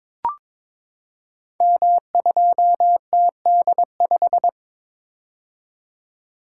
Morse code, heard as a keyed tone, about 23 wpm, 700 Hz. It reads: M2TD5
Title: Code Groups - 5 Letter:Numbers - 22wpm